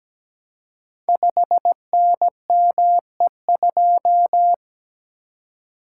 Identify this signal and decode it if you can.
Morse code, heard as a keyed tone, about 17 wpm, 700 Hz. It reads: 5NME2
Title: Code Groups - 5 Letter:Numbers - 17wpm